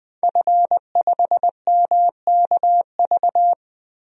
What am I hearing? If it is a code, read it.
Morse code, heard as a keyed tone, about 20 wpm, 700 Hz. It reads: F5MKV
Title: DXCC-100 Call Signs - 5 Characters - Repeated 3x - 20wpm